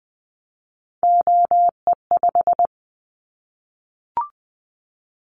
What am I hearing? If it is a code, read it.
Morse code, heard as a keyed tone, about 20 wpm, 700 Hz. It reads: OE5
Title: Callsign Prefixes for Top 50 DXCC - 20wpm